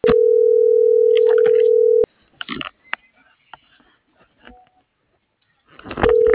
Ringback tone (North America), 440 + 480 Hz.
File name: CENTRAL MULTIMODAL / out kimberly.suarez